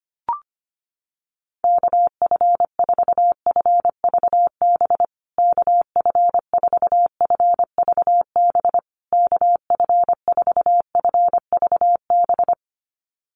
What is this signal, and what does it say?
Morse code, heard as a keyed tone, about 25 wpm, 700 Hz. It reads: KF4FVB KF4FVB KF4FVB
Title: USA Call Signs Repeated 3x 25wpm